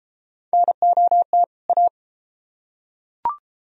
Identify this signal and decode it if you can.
Morse code, heard as a keyed tone, about 33 wpm, 700 Hz. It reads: NOT A